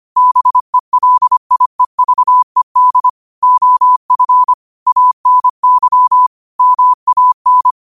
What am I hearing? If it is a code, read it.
Morse code, heard as a keyed tone, about 25 wpm, 1000 Hz. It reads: BELIEVED OF ANY MAN
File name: qotdfiles / 25wpm